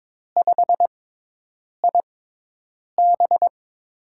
Morse code, heard as a keyed tone, about 22 wpm, spaced as for 8 wpm, 700 Hz. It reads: 5IB
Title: USA Call Signs - 4 Characters - 6x Character Spacing - 22wpm